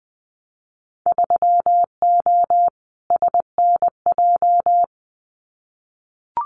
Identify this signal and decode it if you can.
Morse code, heard as a keyed tone, about 20 wpm, 700 Hz. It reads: 3O SNJ
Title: Winter Field Day Exchanges Encore 20wpm